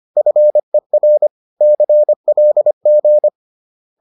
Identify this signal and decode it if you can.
Morse code, heard as a keyed tone, about 25 wpm, 600 Hz. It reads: FER CLG